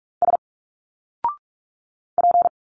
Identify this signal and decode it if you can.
Morse code, heard as a keyed tone, about 42 wpm, 700 Hz. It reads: S P
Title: Single Letters 45wpm